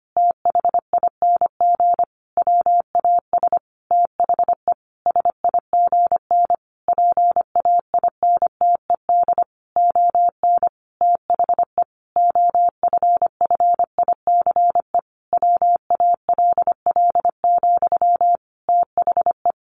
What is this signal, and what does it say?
Morse code, heard as a keyed tone, about 25 wpm, 700 Hz. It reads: THING WAS THE SIGN PAINTED ON THE OFFICE WALL, THE
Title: A Sound Of Thunder 25wpm